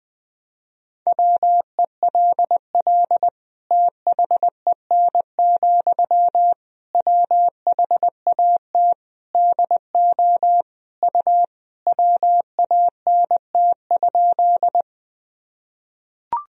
Morse code, heard as a keyed tone, about 20 wpm, 700 Hz. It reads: WELL THEN, WHAT DO U WANT?
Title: Sentences from Top 200 Words and Abbreviations 20wpm